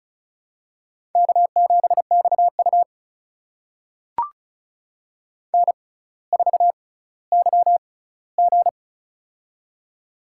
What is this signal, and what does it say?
Morse code, heard as a keyed tone, about 36 wpm, spaced as for 18 wpm, 700 Hz. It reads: K7XU N4YG
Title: USA Call Signs - 4 Characters - 6x Character Spacing - 35wpm